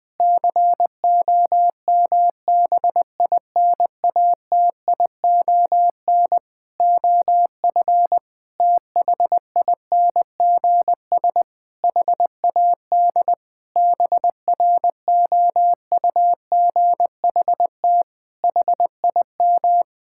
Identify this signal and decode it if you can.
Morse code, heard as a keyed tone, about 20 wpm, 700 Hz. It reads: COMBINATION OF THINGS HAD BROUGHT HIM
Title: Sentences from Top 1000 Words - Review Part 6 - 20wpm